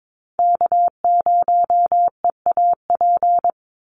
Morse code, heard as a keyed tone, about 22 wpm, 700 Hz. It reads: K0EAP
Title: NA SOTA Chaser Callsigns 22wpm